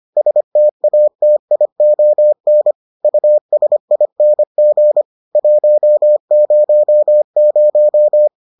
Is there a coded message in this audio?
Yes: Morse code from 0.2 to 8.3 s, heard as a keyed tone, about 25 wpm, 600 Hz. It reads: STATION USING 100